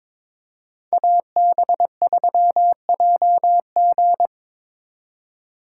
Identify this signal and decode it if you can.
Morse code, heard as a keyed tone, about 26 wpm, 700 Hz. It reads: AB3JG
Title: Speed Racing - USA 2x2 Call Signs 22wpm